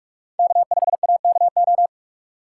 Morse code, heard as a keyed tone, about 44 wpm, 700 Hz. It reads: K5AKO
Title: USA Call Signs - 5 Characters - 4x Character Spacing - 45wpm